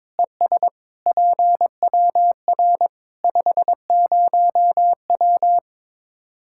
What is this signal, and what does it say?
Morse code, heard as a keyed tone, about 22 wpm, 700 Hz. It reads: ES PWR 50W
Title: Practice QSO - Exchange 2 - Info - 22wpm